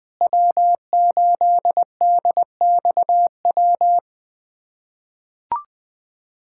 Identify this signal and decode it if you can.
Morse code, heard as a keyed tone, about 20 wpm, 700 Hz. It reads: W8DXW
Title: USA Callsign DE Callsign - 5 Characters - 20wpm